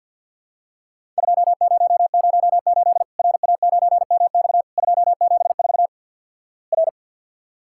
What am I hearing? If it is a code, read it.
Morse code, heard as a keyed tone, about 48 wpm, 700 Hz. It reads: 2008 RA9MX 274 R
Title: Worked All Europe DX Contest - QTC Exchange - 50wpm